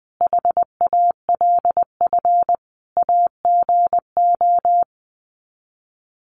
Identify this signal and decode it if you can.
Morse code, heard as a keyed tone, about 20 wpm, 700 Hz. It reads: HALF AGO
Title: Sets of 5 Words Encore 20wpm